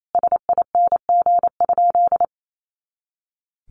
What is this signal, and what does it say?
Morse code, heard as a keyed tone, about 28 wpm, 700 Hz. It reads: SING?
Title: Sentences from Top 400 Words 28wpm